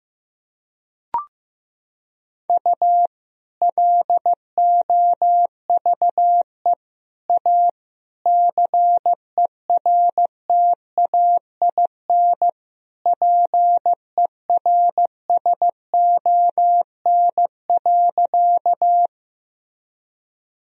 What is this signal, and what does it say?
Morse code, heard as a keyed tone, about 15 wpm, 700 Hz. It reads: U LOVE A CERTAIN PERSON.
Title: Sentences from Top 500 Words and Abbreviations 15wpm